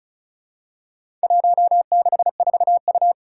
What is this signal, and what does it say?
Morse code, heard as a keyed tone, about 36 wpm, 700 Hz. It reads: 164U
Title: Four Challenging Letter-Number Combinations 35wpm